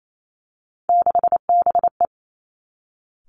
Morse code, heard as a keyed tone, about 28 wpm, 700 Hz. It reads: 6BE
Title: Three Challenging Letter-Number Combinations 28wpm